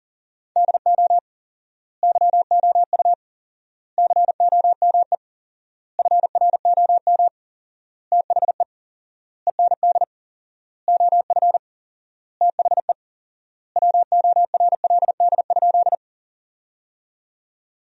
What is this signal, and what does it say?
Morse code, heard as a keyed tone, about 40 wpm, 700 Hz. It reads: DO YOU COME FROM THE END OF THE WORLD?
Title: 4x Word-Spacing - Sentences from Top 200 Words - 40wpm